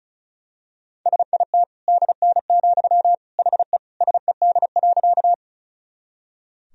Morse code, heard as a keyed tone, about 34 wpm, 700 Hz. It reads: SIT DN, HE SED.
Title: Sentences from Top 500 Words and Abbreviations 35wpm